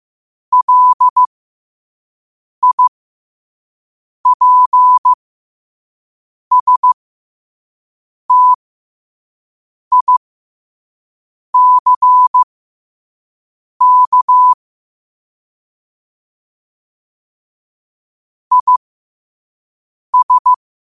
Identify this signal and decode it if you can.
Morse code, heard as a keyed tone, about 15 wpm, spaced as for 5 wpm, 1000 Hz. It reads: LIPSTICK IS
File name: qotdfiles / 5wpm